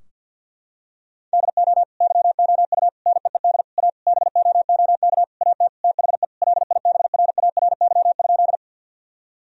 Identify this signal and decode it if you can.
Morse code, heard as a keyed tone, about 50 wpm, 700 Hz. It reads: DO YOU NEED A BOOK AT THE LIBRARY?
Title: Sentences from Top 1000 Words - Review Part 4 - 50wpm